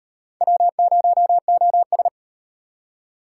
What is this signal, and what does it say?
Morse code, heard as a keyed tone, about 32 wpm, 700 Hz. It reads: W0OS